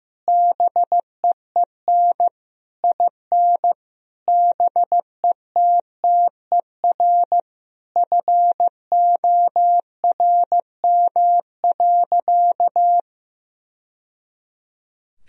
Morse code, heard as a keyed tone, about 15 wpm, 700 Hz. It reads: BEEN IN BETTER FORM.
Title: Sentences from Top 1000 Words - Review Part 1 - 15wpm